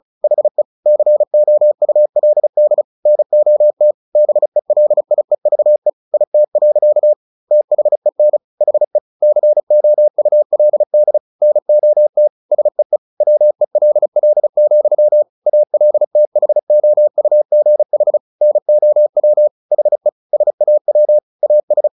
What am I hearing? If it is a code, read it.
Morse code, heard as a keyed tone, about 34 wpm, 600 Hz. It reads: HE COULD NOT BELIEVE IT. THEN HE COULD NOT SEE WELL, ALTHOUGH NOW HE SAW AS
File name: the old man and the sea 2 35wpm